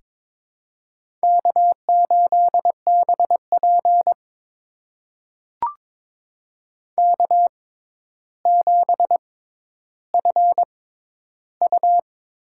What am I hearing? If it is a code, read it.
Morse code, heard as a keyed tone, about 22 wpm, spaced as for 11 wpm, 700 Hz. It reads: K8BP K7FU
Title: USA Call Signs - 4 Characters - 6x Character Spacing - 22wpm